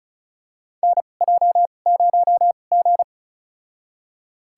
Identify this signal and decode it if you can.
Morse code, heard as a keyed tone, about 34 wpm, spaced as for 25 wpm, 700 Hz. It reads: NJ0G
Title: USA Call Signs - 4 Characters - 2x Character Spacing - 35wpm